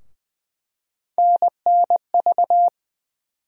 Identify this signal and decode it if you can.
Morse code, heard as a keyed tone, about 21 wpm, 700 Hz. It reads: NNV
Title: USA Call Sign Postfixes - 3 Letters 20wpm